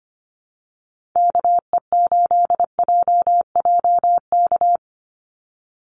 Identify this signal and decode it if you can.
Morse code, heard as a keyed tone, about 25 wpm, 700 Hz. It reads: KE8JJK
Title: USA Call Signs 25wpm